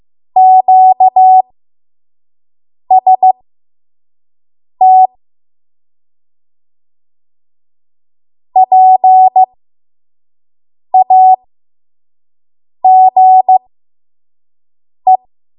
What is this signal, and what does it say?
Morse code, heard as a keyed tone, about 15 wpm, spaced as for 5 wpm, 750 Hz. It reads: QST PAGE
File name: Morse / 05WPM